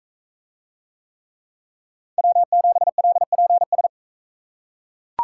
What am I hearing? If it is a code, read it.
Morse code, heard as a keyed tone, about 40 wpm, 700 Hz. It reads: W7LPS